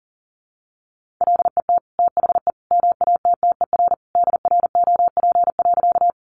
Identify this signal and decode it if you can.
Morse code, heard as a keyed tone, about 40 wpm, 700 Hz. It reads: LET THE MATTER DROP.
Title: Sentences from Top 1400 Words 40wpm